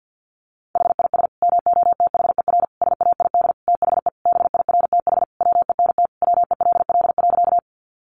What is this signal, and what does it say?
Morse code, heard as a keyed tone, about 50 wpm, 700 Hz. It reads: HIS MOTHER SAID THE BIRTH WENT WELL.
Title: Sentences from Top 800 Words 50wpm